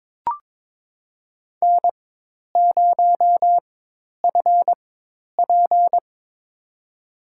Morse code, heard as a keyed tone, about 21 wpm, spaced as for 10 wpm, 700 Hz. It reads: N0FP
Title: USA Call Signs - 4 Characters - 4x Character Spacing - 22wpm